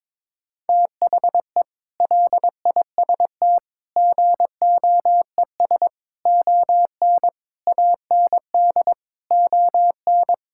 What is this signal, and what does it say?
Morse code, heard as a keyed tone, about 22 wpm, 700 Hz. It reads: THE LIST GOES ON AND ON
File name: practice / 22wpm / Sentences with 8-Word Phrases - 22wpm